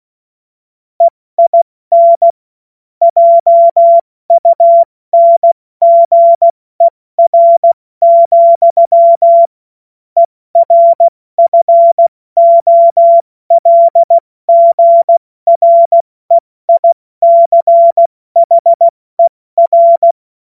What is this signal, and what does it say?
Morse code, heard as a keyed tone, about 16 wpm, 680 Hz. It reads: EIN JUNGER, ERFOLGREICHER
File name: Morsekurs / Witze / Witz 17. Mai 2022 70 F80